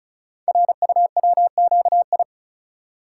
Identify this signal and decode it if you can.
Morse code, heard as a keyed tone, about 36 wpm, 700 Hz. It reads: RUWQI
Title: Code Groups - 5 Letters - 35wpm